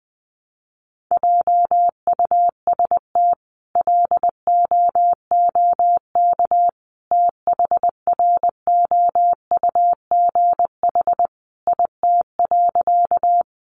Morse code, heard as a keyed tone, about 20 wpm, 700 Hz. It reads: JUST LOOK THROUGH IT.
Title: Sentences from Top 200 Words 20wpm